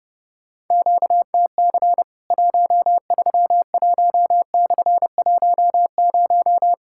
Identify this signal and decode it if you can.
Morse code, heard as a keyed tone, about 32 wpm, 700 Hz. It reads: QTC 131/10
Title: Worked All Europe DX Contest - QTC Exchange - 30wpm